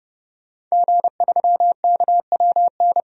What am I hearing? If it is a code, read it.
Morse code, heard as a keyed tone, about 29 wpm, 700 Hz. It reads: G3KWN
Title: DXCC-100 Call Signs - 5 Characters - 30wpm